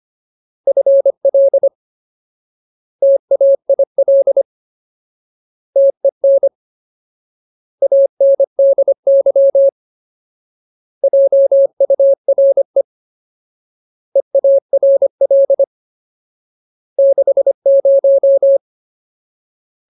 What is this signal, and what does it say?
Morse code, heard as a keyed tone, about 25 wpm, 550 Hz. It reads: FL TAIL TEN ANDY JURE EARL 60
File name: Ultimate Word List / 25WPM Ultimate